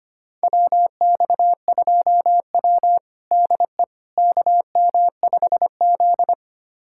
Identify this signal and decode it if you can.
Morse code, heard as a keyed tone, about 25 wpm, 700 Hz. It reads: WX2W DE KM5Z
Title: USA Callsign DE Callsign - 4 Characters - 25wpm